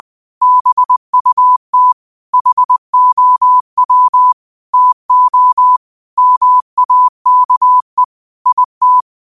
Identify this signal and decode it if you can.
Morse code, heard as a keyed tone, about 20 wpm, 1000 Hz. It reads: BUT HOW TO MAKE IT